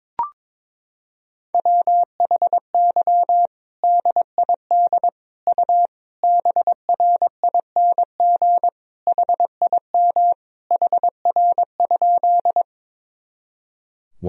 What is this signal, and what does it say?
Morse code, heard as a keyed tone, about 22 wpm, 700 Hz. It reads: WHY DID U BRING HIM HR?
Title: Sentences from Top 500 Words and Abbreviations 22wpm